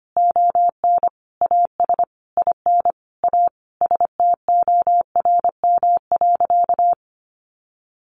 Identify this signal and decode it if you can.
Morse code, heard as a keyed tone, about 25 wpm, 700 Hz. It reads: ON AS IN A STORM.
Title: Sentences from Top 1500 Words 25wpm